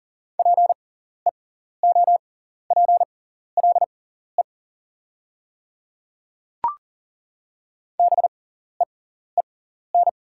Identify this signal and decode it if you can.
Morse code, heard as a keyed tone, about 38 wpm, spaced as for 14 wpm, 700 Hz. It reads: PEOPLE BEEN
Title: 6x Character Spacing - Top 200 Words - 40wpm